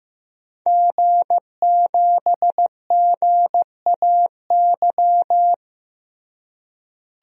Morse code, heard as a keyed tone, about 15 wpm, 700 Hz. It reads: G7GAY
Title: DXCC-100 Call Signs - 5 Characters - Repeated 3x - 15wpm